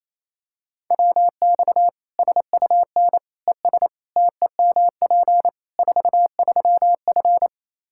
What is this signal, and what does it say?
Morse code, heard as a keyed tone, about 28 wpm, 700 Hz. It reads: WX SUN ES TEMP 43F
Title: Practice QSO - Exchange 2 - Info - 28wpm